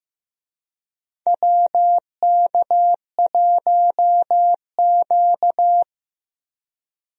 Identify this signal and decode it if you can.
Morse code, heard as a keyed tone, about 15 wpm, 700 Hz. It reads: WK1Q
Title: USA Call Signs - 4 Characters - Repeated 2x - 15wpm